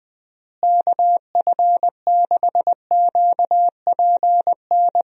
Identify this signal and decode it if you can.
Morse code, heard as a keyed tone, about 20 wpm, 700 Hz. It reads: KF6QPN
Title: USA Call Signs - 6 Characters - Repeated 2x - 20wpm